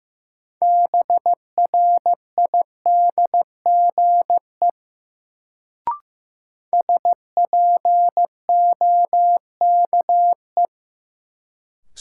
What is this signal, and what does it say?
Morse code, heard as a keyed tone, about 16 wpm, 700 Hz. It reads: BRIDGE SPOKE
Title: Sentences from Top 1200 Words - New Word Drill - 15wpm